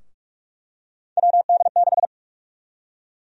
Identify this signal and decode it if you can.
Morse code, heard as a keyed tone, about 47 wpm, 700 Hz. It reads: WD6
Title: USA Call Sign Prefixes - Repeated 2x - 45wpm